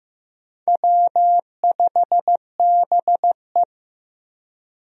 Morse code, heard as a keyed tone, about 15 wpm, 700 Hz. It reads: W5BE
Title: USA Call Signs - 4 Characters - Repeated 2x - 15wpm